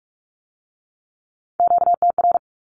Morse code, heard as a keyed tone, about 46 wpm, 700 Hz. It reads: QTR
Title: Q-Codes with Spoken Meaning - 45wpm